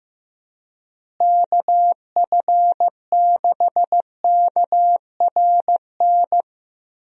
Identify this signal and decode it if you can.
Morse code, heard as a keyed tone, about 15 wpm, 700 Hz. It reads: KF6KRN
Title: USA Call Signs - 6 Characters - Repeated 2x - 15wpm